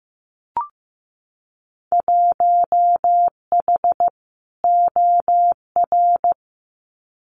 Morse code, heard as a keyed tone, about 15 wpm, 700 Hz. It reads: 1H OR
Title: Winter Field Day Exchanges 15wpm